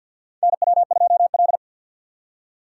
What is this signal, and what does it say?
Morse code, heard as a keyed tone, about 53 wpm, 700 Hz. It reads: NW2L
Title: USA Call Signs - 4 Characters - 3x Character Spacing - 50wpm